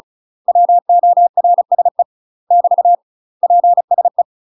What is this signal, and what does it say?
Morse code, heard as a keyed tone, about 34 wpm, 700 Hz. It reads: WORSE <BT> PSE
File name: UK MASTER QSOs TRAINING / 35WPM / UK MASTER QSOs 0039